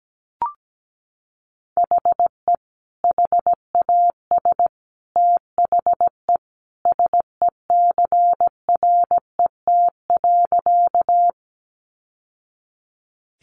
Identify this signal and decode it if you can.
Morse code, heard as a keyed tone, about 17 wpm, 700 Hz. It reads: HE HAS THE SECRET.
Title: Sentences from Top 1200 Words 17wpm